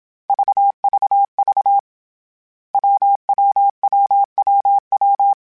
Morse code, heard as a keyed tone, about 27 wpm, 800 Hz. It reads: VVV WWWWW
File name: Difficult Combos / Difficult Combos-mp3s / WPJ WPJ1 @25